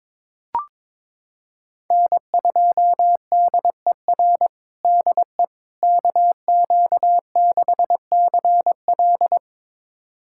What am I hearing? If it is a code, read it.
Morse code, heard as a keyed tone, about 22 wpm, 700 Hz. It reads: N2DER DE KQ6CL
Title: USA Callsign DE Callsign - 5 Characters - 22wpm